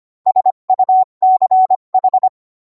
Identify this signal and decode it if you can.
Morse code, heard as a keyed tone, about 25 wpm, 750 Hz. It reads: SUCH